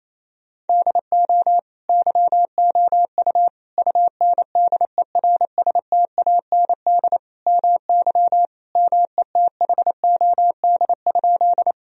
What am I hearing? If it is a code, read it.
Morse code, heard as a keyed tone, about 28 wpm, 700 Hz. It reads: DO YOU UNDERSTAND MY METHOD?
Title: Sentences from Top 1000 Words - Review Part 3 - 28wpm